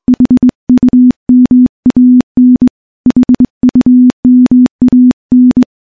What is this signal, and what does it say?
Morse code, heard as a keyed tone, about 20 wpm, 250 Hz. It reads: HUMAN HUMAN
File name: morse / morse-training-20WPM